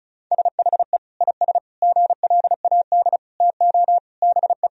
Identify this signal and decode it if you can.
Morse code, heard as a keyed tone, about 36 wpm, 700 Hz. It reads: SHE IS GLAD TO BE
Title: Sentences from Top 1000 Words - Review Part 2 - 35wpm